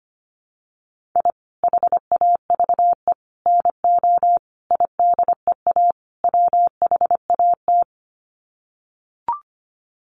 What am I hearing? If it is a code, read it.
Morse code, heard as a keyed tone, about 25 wpm, 700 Hz. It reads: I HAVE NO IDEA WHAT
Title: Sets of 5 Words Encore 25wpm